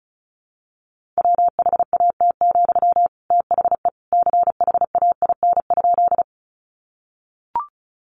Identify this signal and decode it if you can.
Morse code, heard as a keyed tone, about 34 wpm, 700 Hz. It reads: WHAT, THE CHAIN?
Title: Sentences from Top 1500 Words 35wpm